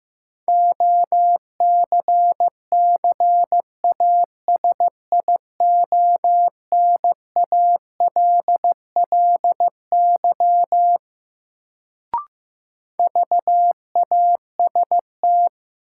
Morse code, heard as a keyed tone, about 15 wpm, 700 Hz. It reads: OCCASIONALLY VAST
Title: Sentences from Top 1300 Words - New Word Drill - 15wpm